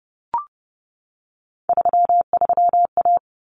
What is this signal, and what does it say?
Morse code, heard as a keyed tone, about 30 wpm, 700 Hz. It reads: 33A
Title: Common Cut Numbers - 3-Digit 30wpm